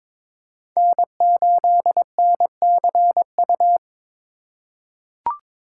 Morse code, heard as a keyed tone, about 22 wpm, 700 Hz. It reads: N8NCU
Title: USA Call Signs - 5 Characters - 2x Character Spacing - 22wpm